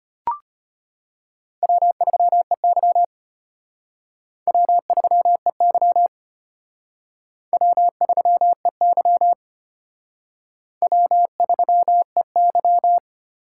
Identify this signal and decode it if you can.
Morse code, heard as a keyed tone, about 34 wpm, 700 Hz. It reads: W3EY W3EY W3EY W3EY